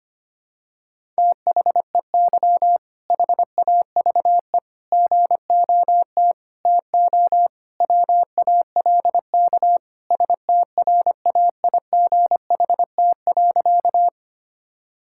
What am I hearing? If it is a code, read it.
Morse code, heard as a keyed tone, about 25 wpm, 700 Hz. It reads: THEY HAVE GOT TO WALK STRAIGHT.
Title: Sentences from Top 1000 Words - Review Part 6 - 25wpm